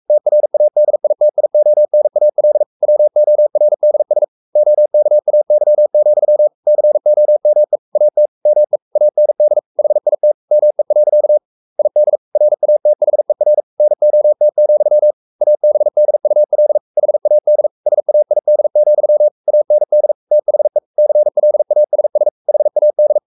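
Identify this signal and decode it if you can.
Morse code, heard as a keyed tone, about 44 wpm, 600 Hz. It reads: TRADITIONAL WORDS OKAY, COME AT ME AND HIT ME. ID RATHER NOT, ABDUL HAD SAID, AND THE CLASS HAD
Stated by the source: Morse code at 43 wpm sending Ken Follett's novel Never in English, from CW med Gnister.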